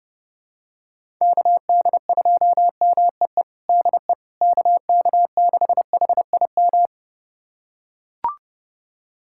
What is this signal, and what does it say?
Morse code, heard as a keyed tone, about 30 wpm, 700 Hz. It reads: KD2MEE DE KK6HIM
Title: USA Callsign DE Callsign - 6 Characters - 30wpm